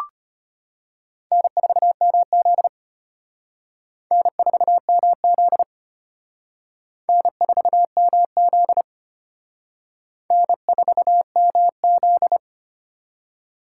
Morse code, heard as a keyed tone, about 32 wpm, 700 Hz. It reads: N4MZ N4MZ N4MZ N4MZ